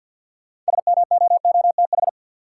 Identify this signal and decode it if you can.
Morse code, heard as a keyed tone, about 50 wpm, 700 Hz. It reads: SMOOTH
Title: Sentences from Top 1400 Words - New Word Drill Alt - 50wpm